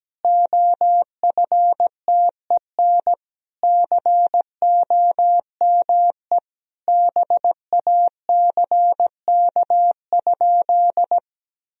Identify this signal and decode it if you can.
Morse code, heard as a keyed tone, about 17 wpm, 700 Hz. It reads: OFTEN COME BACK?
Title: Sentences from Top 300 Words and Abbreviations 17wpm